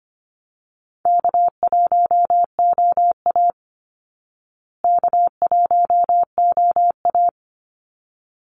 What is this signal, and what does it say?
Morse code, heard as a keyed tone, about 25 wpm, 700 Hz. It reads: K1OA K1OA
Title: Call Signs Repeated 3x - 25wpm